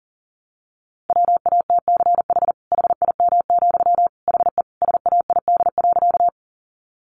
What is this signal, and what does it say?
Morse code, heard as a keyed tone, about 40 wpm, 700 Hz. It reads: WATCH HIM, HE SAID.
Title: Sentences from Top 300 Words 40wpm